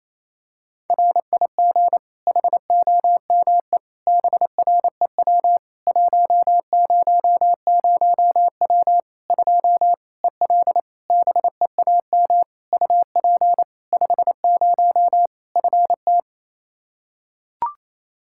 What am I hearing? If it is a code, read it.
Morse code, heard as a keyed tone, about 27 wpm, 700 Hz. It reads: RIG HOME BREW 100W 2 EL BEAM UP 50 FT
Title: Practice QSO - Exchange 2 - Info - 28wpm